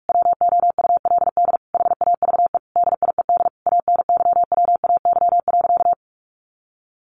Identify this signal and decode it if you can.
Morse code, heard as a keyed tone, about 44 wpm, 700 Hz. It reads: WOULD HAVE DIED ANYWAY.
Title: A Sound Of Thunder 45wpm